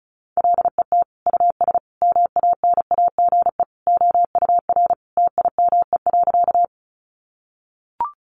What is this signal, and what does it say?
Morse code, heard as a keyed tone, about 34 wpm, 700 Hz. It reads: LET US MANAGE OUR TIME.
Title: Sentences from Top 1400 Words 35wpm